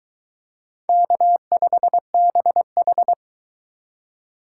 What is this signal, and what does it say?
Morse code, heard as a keyed tone, about 23 wpm, 700 Hz. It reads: K5BH